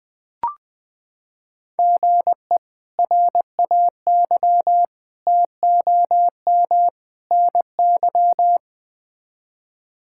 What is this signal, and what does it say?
Morse code, heard as a keyed tone, about 20 wpm, 700 Hz. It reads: GE RAY TOM NY
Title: K1USN SST QSO Practice - 20wpm